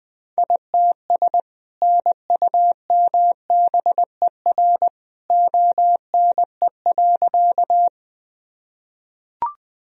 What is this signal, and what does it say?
Morse code, heard as a keyed tone, about 21 wpm, 700 Hz. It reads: ITS NUMBER ONE.
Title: Sentences from Top 1000 Words - Review Part 1 - 20wpm